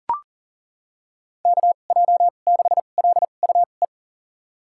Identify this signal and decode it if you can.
Morse code, heard as a keyed tone, about 40 wpm, spaced as for 29 wpm, 700 Hz. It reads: KJ6LUE
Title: USA Call Signs - 6 Characters - 2x Character Spacing - 40wpm